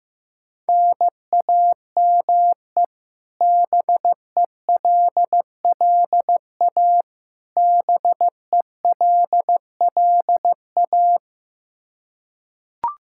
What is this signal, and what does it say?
Morse code, heard as a keyed tone, about 15 wpm, 700 Hz. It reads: NAME BELLA BELLA
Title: Practice QSO - Exchange 1 - Beginning - 15wpm